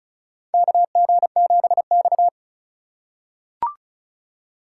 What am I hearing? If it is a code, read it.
Morse code, heard as a keyed tone, about 36 wpm, 700 Hz. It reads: KG7X
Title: USA Call Signs - 4 Characters - 3x Character Spacing - 35wpm